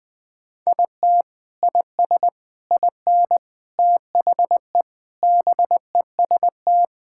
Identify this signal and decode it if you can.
Morse code, heard as a keyed tone, about 20 wpm, 700 Hz. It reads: IT IS IN THE BEST
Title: Sentences with 6-Word Phrases - Spoken Beginning - 20wpm